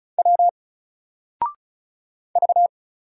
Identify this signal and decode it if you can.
Morse code, heard as a keyed tone, about 36 wpm, 700 Hz. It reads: W V
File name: Single-Letter-Number-35wpm